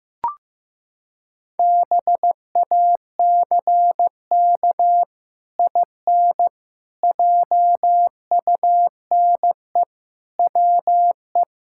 Morse code, heard as a keyed tone, about 15 wpm, 700 Hz. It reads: BACK IN JUNE WE
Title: Pangram Challenge v2 - 15wpm